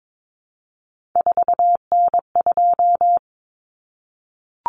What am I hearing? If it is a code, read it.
Morse code, heard as a keyed tone, about 22 wpm, 700 Hz. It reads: 4N2
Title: Common Cut Numbers - 3-Digit 22wpm